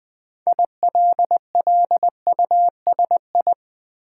Morse code, heard as a keyed tone, about 20 wpm, 700 Hz. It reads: ILLUSI
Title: Build A Word - 2 Letters At A Time - 8-Letter Words - 20wpm